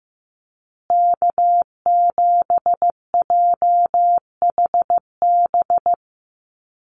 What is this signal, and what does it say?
Morse code, heard as a keyed tone, about 15 wpm, 700 Hz. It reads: K7JHB
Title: USA Call Signs Repeated 2x 15wpm